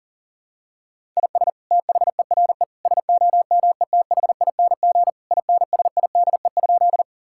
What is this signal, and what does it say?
Morse code, heard as a keyed tone, about 37 wpm, 700 Hz. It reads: IS THERE SOMETHING INSIDE?
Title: Sentences from Top 1000 Words - Review Part 6 - 40wpm